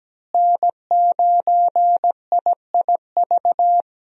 Morse code, heard as a keyed tone, about 17 wpm, 700 Hz. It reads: N9IIV
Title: USA Call Signs - 5 Characters - Repeated 2x - 17wpm